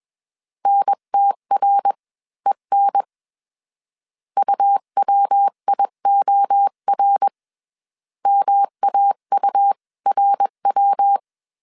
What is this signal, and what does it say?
Morse code, heard as a keyed tone, about 21 wpm, spaced as for 18 wpm, 800 Hz. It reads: NTREN UWIOR MAURW